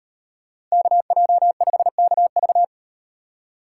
Morse code, heard as a keyed tone, about 38 wpm, 700 Hz. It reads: KJ5KV